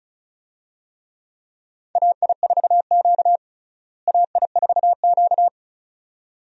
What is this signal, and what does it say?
Morse code, heard as a keyed tone, about 36 wpm, 700 Hz. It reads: AI4Q AI4Q
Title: USA Call Signs - 4 Characters - Repeated 2x - 35wpm